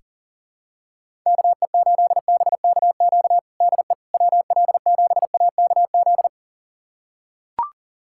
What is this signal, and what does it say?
Morse code, heard as a keyed tone, about 40 wpm, 700 Hz. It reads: KE8BKQ DE WL7AKZ
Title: USA Callsign DE Callsign - 6 Characters - 40wpm